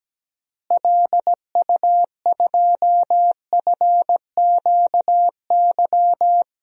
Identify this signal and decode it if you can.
Morse code, heard as a keyed tone, about 17 wpm, 700 Hz. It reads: LU2FQY
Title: DXCC-100 Call Signs - 6 Characters - Repeated 2x - 17wpm